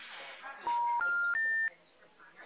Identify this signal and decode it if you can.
Special information tone, with noise.
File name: out- -2022- - - .54